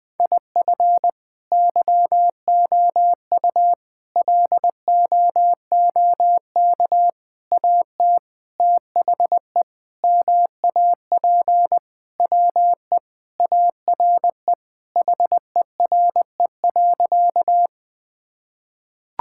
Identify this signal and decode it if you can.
Morse code, heard as a keyed tone, about 20 wpm, 700 Hz. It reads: IF YOU LOOK AT THE MAP WE ARE HERE.
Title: Sentences with 5-Word Phrases - 20wpm